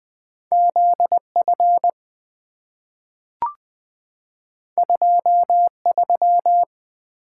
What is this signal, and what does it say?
Morse code, heard as a keyed tone, about 20 wpm, 700 Hz. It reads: ZF 23
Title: Two Letter-Number Combinations - 20wpm